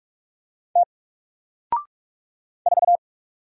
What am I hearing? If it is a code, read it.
Morse code, heard as a keyed tone, about 44 wpm, 700 Hz. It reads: T 4
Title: Single Letter-Number Alt 45wpm